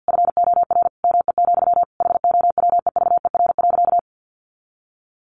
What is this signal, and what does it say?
Morse code, heard as a keyed tone, about 50 wpm, 700 Hz. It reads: FOR ME, HOWEVER.
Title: Sentences from Top 600 Words 50wpm